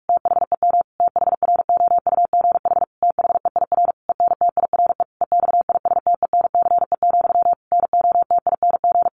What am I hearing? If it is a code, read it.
Morse code, heard as a keyed tone, about 44 wpm, 700 Hz. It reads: THEM THROUGH THEIR ENTIRE EXISTENCE, NOTING
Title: A Sound Of Thunder 45wpm